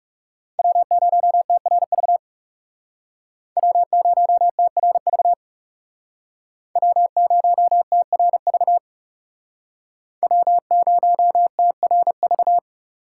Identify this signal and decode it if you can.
Morse code, heard as a keyed tone, about 38 wpm, 700 Hz. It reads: W0TRV W0TRV W0TRV W0TRV